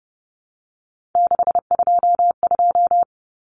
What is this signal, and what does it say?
Morse code, heard as a keyed tone, about 30 wpm, 700 Hz. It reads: 622
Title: Common Cut Numbers - 3-Digit 30wpm